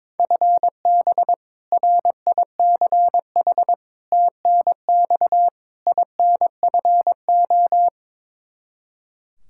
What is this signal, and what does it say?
Morse code, heard as a keyed tone, about 22 wpm, 700 Hz. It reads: FB RICH TNX INFO
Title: Practice QSO - Exchange 3 - More Info - 22wpm